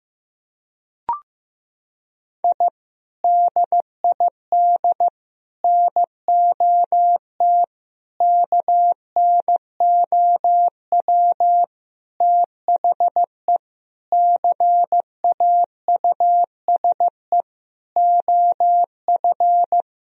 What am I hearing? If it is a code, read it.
Morse code, heard as a keyed tone, about 15 wpm, 700 Hz. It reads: I DID NOT KNOW THE CAUSE OF
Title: Sentences from Top 1000 Words - Review Part 1 - 15wpm